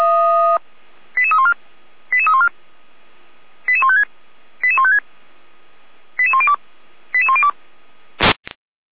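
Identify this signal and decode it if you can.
Five-tone selective call (ZVEI), heard as quick runs of five tones, tones of about 70 ms. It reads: 88314 88314 88156 88156 88102 88102
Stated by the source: ZVEI five-tone selective call 88192, received by SDR.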